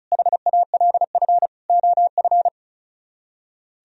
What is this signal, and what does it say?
Morse code, heard as a keyed tone, about 44 wpm, 700 Hz. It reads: HALF OF